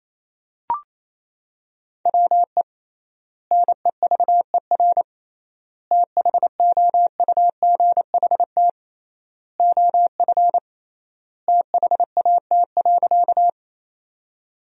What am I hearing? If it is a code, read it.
Morse code, heard as a keyed tone, about 28 wpm, 700 Hz. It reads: WE NEVER THOUGHT OF THAT.
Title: 3x Word-Spacing - Sentences from Top 300 Words - 28wpm